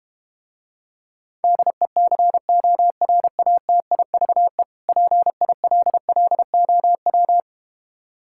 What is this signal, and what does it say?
Morse code, heard as a keyed tone, about 32 wpm, 700 Hz. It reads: DECORATIVE PILLOW